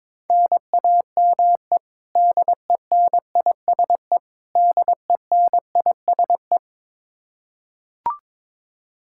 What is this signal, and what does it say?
Morse code, heard as a keyed tone, about 22 wpm, 700 Hz. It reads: NAME DENISE DENISE
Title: Practice QSO - Exchange 1 - Beginning - 22wpm